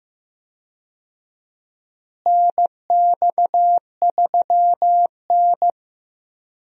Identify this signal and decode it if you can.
Morse code, heard as a keyed tone, about 15 wpm, 700 Hz. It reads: NX3N
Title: USA Call Signs - 4 Characters - Repeated 2x - 15wpm